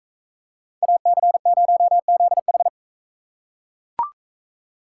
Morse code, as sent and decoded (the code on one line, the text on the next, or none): .- -.-. ----- --.. ....
AC0ZH